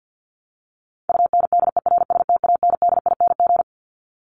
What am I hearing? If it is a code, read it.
Morse code, heard as a keyed tone, about 50 wpm, 700 Hz. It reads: UNDERSTANDING
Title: Top 501-750 Words 50wpm